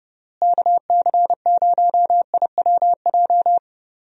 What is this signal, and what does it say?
Morse code, heard as a keyed tone, about 30 wpm, 700 Hz. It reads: KC0IWJ
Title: USA Call Signs - 6 Characters - Repeated 3x - 30wpm